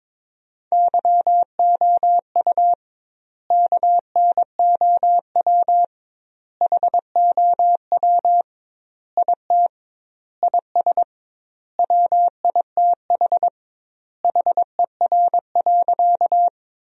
Morse code, heard as a keyed tone, about 23 wpm, 700 Hz. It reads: YOU KNOW HOW IT IS WITH HER.
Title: 2x Word-Spacing - Sentences from Top 100 Words - 22wpm